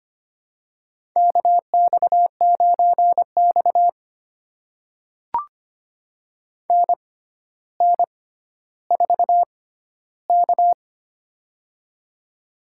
Morse code, heard as a keyed tone, about 25 wpm, spaced as for 13 wpm, 700 Hz. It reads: KX9X NN4K
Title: USA Call Signs - 4 Characters - 6x Character Spacing - 25wpm